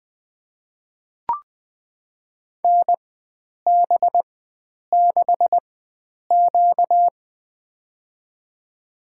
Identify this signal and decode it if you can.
Morse code, heard as a keyed tone, about 20 wpm, spaced as for 9 wpm, 700 Hz. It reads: NB6Q
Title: USA Call Signs - 4 Characters - 4x Character Spacing - 20wpm